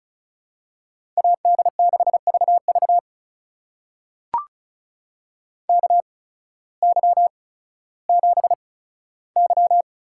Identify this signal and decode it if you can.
Morse code, heard as a keyed tone, about 36 wpm, spaced as for 16 wpm, 700 Hz. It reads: AD6VV KY7Y
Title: USA Call Signs - 8x Character Spacing - 35wpm